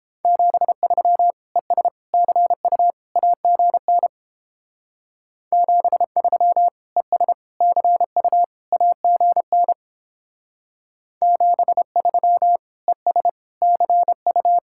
Morse code, heard as a keyed tone, about 29 wpm, 700 Hz. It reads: 73 ES CU AGN 73 ES CU AGN 73 ES CU